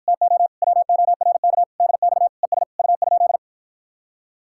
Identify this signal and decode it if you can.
Morse code, heard as a keyed tone, about 52 wpm, 700 Hz. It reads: TO WORK DX ES U?